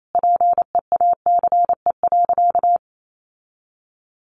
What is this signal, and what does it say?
Morse code, heard as a keyed tone, about 28 wpm, 700 Hz. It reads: PEACE.
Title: Sentences from Top 1500 Words 28wpm